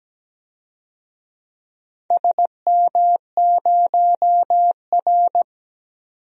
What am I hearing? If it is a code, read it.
Morse code, heard as a keyed tone, about 17 wpm, 700 Hz. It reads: SM0R
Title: DXCC-100 Call Signs - 4 Characters - 17wpm